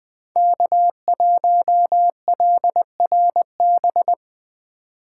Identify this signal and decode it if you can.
Morse code, heard as a keyed tone, about 20 wpm, 700 Hz. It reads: K1LRB
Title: USA Call Signs - 5 Characters - 2x Character Spacing - 20wpm